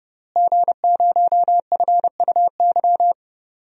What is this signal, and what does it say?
Morse code, heard as a keyed tone, about 30 wpm, 700 Hz. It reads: G0FUY
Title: DXCC-100 Call Signs - 5 Characters - Repeated 2x - 30wpm